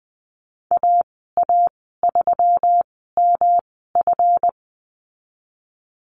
Morse code, heard as a keyed tone, about 20 wpm, spaced as for 15 wpm, 700 Hz. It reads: AA3MF
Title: 173 Lesson 28g - Number 9 - Call Signs 20wpm